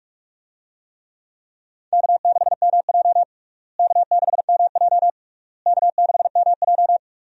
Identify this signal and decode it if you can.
Morse code, heard as a keyed tone, about 46 wpm, 700 Hz. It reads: K6MJ K6MJ K6MJ
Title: USA Call Signs - 4 Characters - Repeated 3x - 45wpm